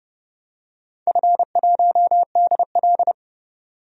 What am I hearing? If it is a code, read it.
Morse code, heard as a keyed tone, about 31 wpm, 700 Hz. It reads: F1DL
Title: DXCC-100 Call Signs - 4 Characters - 30wpm